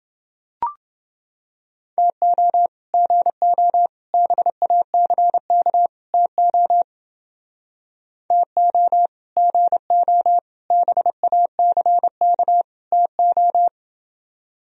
Morse code, heard as a keyed tone, about 29 wpm, 700 Hz. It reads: TO GO BACK TO TO GO BACK TO